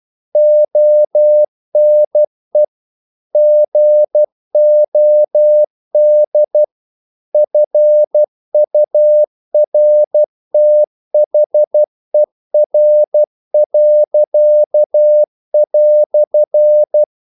Morse code, heard as a keyed tone, about 12 wpm, 600 Hz. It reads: ONE GOD FURTHER."
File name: CW / Practice / EnglishQuotations0112